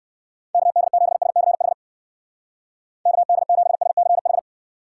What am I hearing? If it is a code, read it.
Morse code, heard as a keyed tone, about 56 wpm, 700 Hz. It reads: KD7SCH KD7SCH
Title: Speed Racing - USA 2x3 Call Signs 45wpm